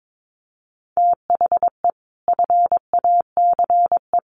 Morse code, heard as a keyed tone, about 22 wpm, 700 Hz. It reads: THE FACE
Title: Sets of 4 Words 22wpm